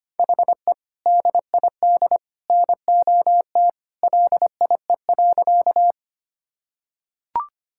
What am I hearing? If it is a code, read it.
Morse code, heard as a keyed tone, about 25 wpm, 700 Hz. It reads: HE DID NOT LIE.
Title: Sentences from Top 1600 Words 25wpm